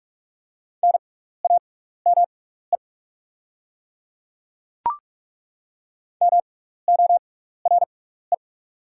Morse code, heard as a keyed tone, about 43 wpm, spaced as for 15 wpm, 700 Hz. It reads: NAME MORE
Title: 6x Character Spacing - Top 200 Words - 45wpm